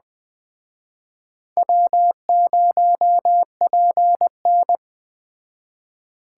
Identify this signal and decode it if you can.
Morse code, heard as a keyed tone, about 21 wpm, 700 Hz. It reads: W0PN